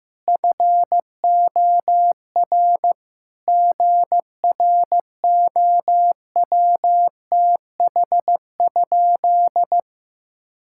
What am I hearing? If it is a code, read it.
Morse code, heard as a keyed tone, about 15 wpm, 700 Hz. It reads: FOR GROWTH?
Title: Sentences from Top 1000 Words - Review Part 5 - 15wpm